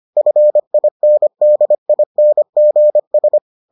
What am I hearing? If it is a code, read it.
Morse code, heard as a keyed tone, about 25 wpm, 600 Hz. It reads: FINDINGS